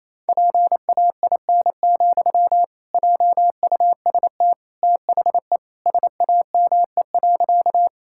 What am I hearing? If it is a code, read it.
Morse code, heard as a keyed tone, about 28 wpm, 700 Hz. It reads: PAIN, JUST THE SAME.
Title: Sentences from Top 1600 Words 28wpm